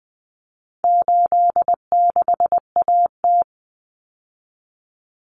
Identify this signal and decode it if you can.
Morse code, heard as a keyed tone, about 20 wpm, 700 Hz. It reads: 86AT
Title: Common Cut Numbers - 4-Digit 20wpm